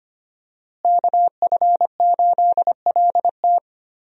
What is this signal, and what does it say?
Morse code, heard as a keyed tone, about 25 wpm, 700 Hz. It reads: KF8LT
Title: USA Call Signs - 5 Characters - Repeated 3x - 25wpm